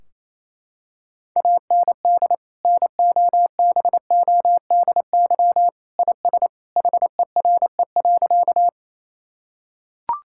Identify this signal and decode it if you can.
Morse code, heard as a keyed tone, about 28 wpm, 700 Hz. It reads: AND NOBODY IS HERE.
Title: Sentences from Top 1600 Words 28wpm